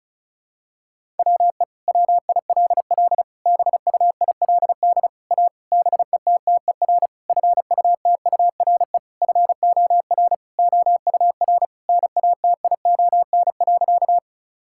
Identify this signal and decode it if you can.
Morse code, heard as a keyed tone, about 34 wpm, 700 Hz. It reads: WE WILL BUILD A BETTER FUTURE FOR OUR NATION.
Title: Sentences from Top 1000 Words - Review Part 6 - 35wpm